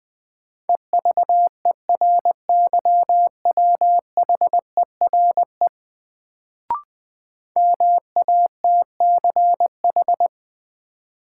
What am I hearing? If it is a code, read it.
Morse code, heard as a keyed tone, about 21 wpm, 700 Hz. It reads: EVERYWHERE MATCH
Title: Sentences from Top 1600 Words - New Word Drill Alt - 20wpm